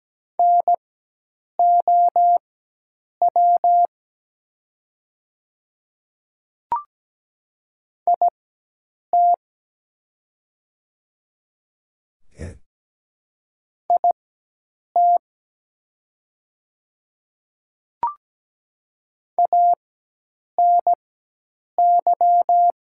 Morse code, heard as a keyed tone, about 17 wpm, spaced as for 8 wpm, 700 Hz. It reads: NOW IT IT ANY
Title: 4x Character Spacing - Top 300 Words - 17wpm